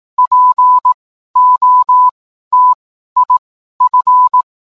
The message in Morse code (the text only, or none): POTIF